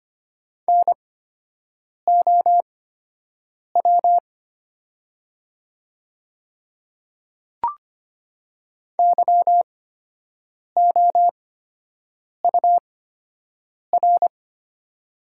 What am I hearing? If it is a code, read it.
Morse code, heard as a keyed tone, about 25 wpm, spaced as for 7 wpm, 690 Hz. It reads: NOW YOUR
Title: 8x Character Spacing - Top 100 Words - 25wpm